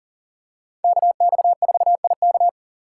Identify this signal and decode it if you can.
Morse code, heard as a keyed tone, about 38 wpm, 700 Hz. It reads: KX4IK
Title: USA Call Signs - 5 Characters - 6x Character Spacing - 40wpm